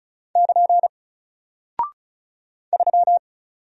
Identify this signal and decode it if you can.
Morse code, heard as a keyed tone, about 36 wpm, 700 Hz. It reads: <KN> 3
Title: Single Character (Letter:Number:Punctuation:Prosign) 35wpm